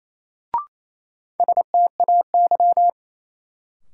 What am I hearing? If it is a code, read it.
Morse code, heard as a keyed tone, about 28 wpm, 700 Hz. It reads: STAY
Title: Sentences from Top 500 Words - New Word Drill - 28wpm